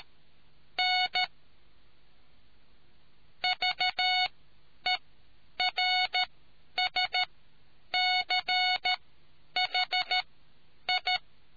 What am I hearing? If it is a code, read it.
Morse code, heard as a keyed tone, about 13 wpm, spaced as for 9 wpm, 740 Hz. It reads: N VERSCHI